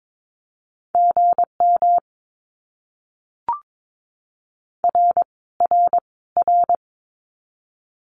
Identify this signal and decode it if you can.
Morse code, heard as a keyed tone, about 22 wpm, 700 Hz. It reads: GM R R R
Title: Abbreviations Repeated 3x - 22wpm